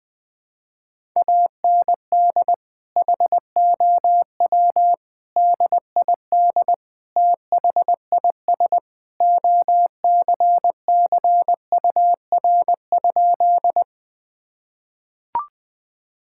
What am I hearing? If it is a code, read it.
Morse code, heard as a keyed tone, about 21 wpm, 700 Hz. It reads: AND HOW DID THIS OCCUR?
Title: Sentences from Top 1000 Words - Review Part 6 - 20wpm